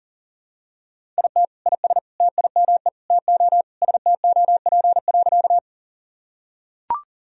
Morse code, heard as a keyed tone, about 40 wpm, 700 Hz. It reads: IT IS TIME TO STOP.
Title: Sentences from Top 1000 Words - Review Part 1 - 40wpm